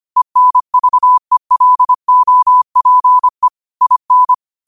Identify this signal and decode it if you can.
Morse code, heard as a keyed tone, about 25 wpm, 1000 Hz. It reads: ENVELOPE IN
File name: qotdfiles / 25wpm